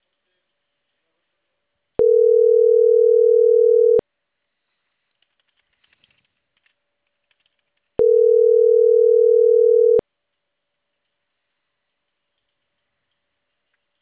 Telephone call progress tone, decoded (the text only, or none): ringback tone